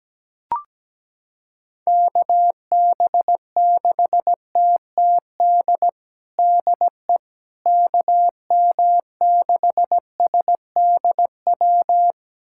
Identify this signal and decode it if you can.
Morse code, heard as a keyed tone, about 17 wpm, 700 Hz. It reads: KB6TTD DE KM6SDW
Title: USA Callsign DE Callsign - 6 Characters - 17wpm